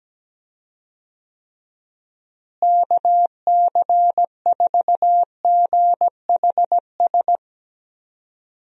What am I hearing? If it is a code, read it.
Morse code, heard as a keyed tone, about 17 wpm, 700 Hz. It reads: KC4GHS
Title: USA Call Signs - 6 Characters - Repeated 2x - 17wpm